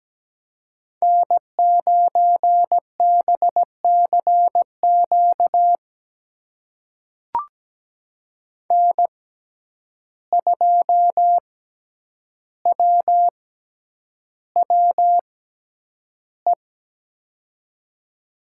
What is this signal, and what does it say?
Morse code, heard as a keyed tone, about 17 wpm, spaced as for 9 wpm, 700 Hz. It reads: N9BCQ N2WWE
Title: USA Call Signs - 5 Characters - 6x Character Spacing - 17wpm